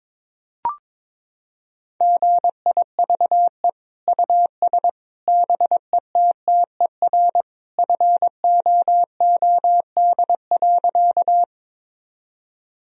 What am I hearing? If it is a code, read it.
Morse code, heard as a keyed tone, about 22 wpm, 700 Hz. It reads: GIVE US BETTER FOOD.
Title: Sentences from Top 1000 Words - Review Part 1 - 22wpm